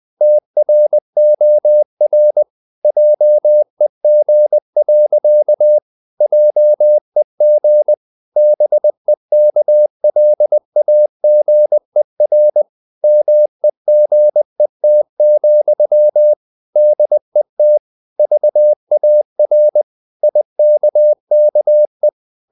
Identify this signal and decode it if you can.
Morse code, heard as a keyed tone, about 20 wpm, 600 Hz. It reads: TROR JEG. JEG BEKLAGER MEGET, DET VAR IKKE
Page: Ferie 20 wpm